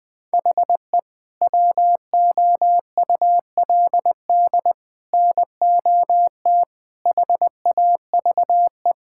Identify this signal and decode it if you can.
Morse code, heard as a keyed tone, about 20 wpm, 700 Hz. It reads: HE WOULD NOT HAVE
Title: Sentences with 3-Word Phrases - 20wpm